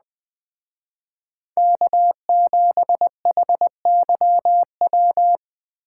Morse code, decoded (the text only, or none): K7HYW